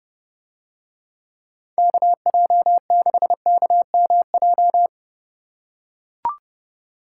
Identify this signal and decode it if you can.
Morse code, heard as a keyed tone, about 30 wpm, 700 Hz. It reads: KJ6KMJ